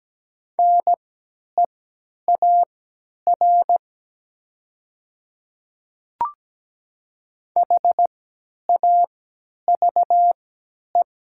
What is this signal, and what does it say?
Morse code, heard as a keyed tone, about 17 wpm, spaced as for 10 wpm, 700 Hz. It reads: NEAR HAVE
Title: 3x Character Spacing - Top 200 Words - 17wpm